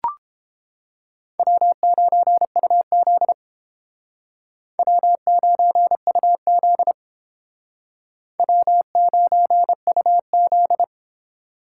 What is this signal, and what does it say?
Morse code, heard as a keyed tone, about 31 wpm, 700 Hz. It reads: W9UZ W9UZ W9UZ